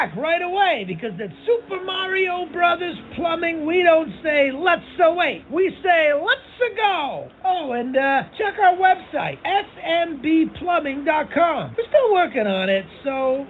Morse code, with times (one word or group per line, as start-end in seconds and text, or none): none